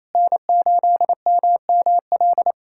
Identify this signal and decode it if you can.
Morse code, heard as a keyed tone, about 27 wpm, 700 Hz. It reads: N8MML
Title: USA Callsign DE Callsign 28wpm